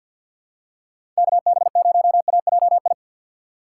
Morse code, heard as a keyed tone, about 52 wpm, 700 Hz. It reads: KB0AJI
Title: USA Call Signs - 6 Characters - 4x Character Spacing - 50wpm